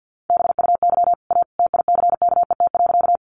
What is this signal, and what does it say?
Morse code, heard as a keyed tone, about 50 wpm, 700 Hz. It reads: BUY A TICKET.
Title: Sentences from Top 1400 Words 50wpm